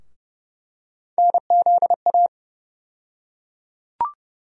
Morse code, heard as a keyed tone, about 30 wpm, 700 Hz. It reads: NZA
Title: USA Call Sign Postfixes - 3 Letters 30wpm